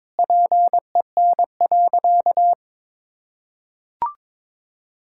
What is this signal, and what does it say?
Morse code, heard as a keyed tone, about 22 wpm, 700 Hz. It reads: PEN.
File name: practice / 22wpm / Easy Sentences with Hints - 22wpm